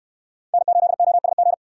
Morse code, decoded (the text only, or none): N8OSG